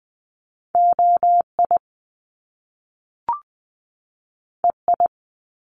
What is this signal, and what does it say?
Morse code, heard as a keyed tone, about 20 wpm, 700 Hz. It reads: OI EI
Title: Top Two Letter Combinations in Words 20wpm